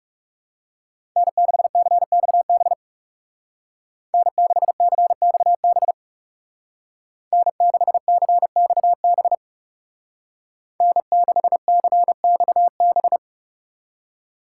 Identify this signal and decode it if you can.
Morse code, heard as a keyed tone, about 35 wpm, 700 Hz. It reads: N6CXB N6CXB N6CXB N6CXB
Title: Speed Racing - USA 1x3 Call Signs 30wpm